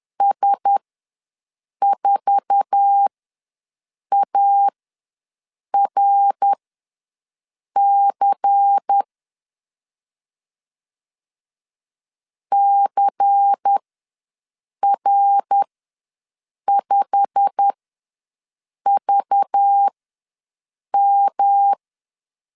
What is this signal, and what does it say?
Morse code, heard as a keyed tone, about 10 wpm, spaced as for 6 wpm, 800 Hz. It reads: S4ARC CR5VM
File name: LEZIONI / 9 2 ws 10wpm 4130